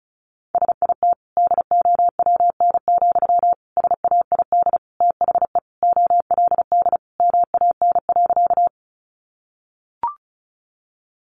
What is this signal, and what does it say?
Morse code, heard as a keyed tone, about 34 wpm, 700 Hz. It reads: SIT DOWN, SAID THE OLD MAN.
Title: Sentences from Top 500 Words 35wpm